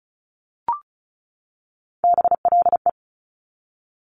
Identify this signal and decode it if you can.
Morse code, heard as a keyed tone, about 36 wpm, 700 Hz. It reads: BLE
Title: Top Three Letter Combinations in Words 35wpm